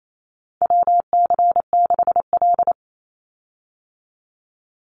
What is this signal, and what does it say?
Morse code, heard as a keyed tone, about 28 wpm, 700 Hz. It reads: WC6L
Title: USA Call Signs 28wpm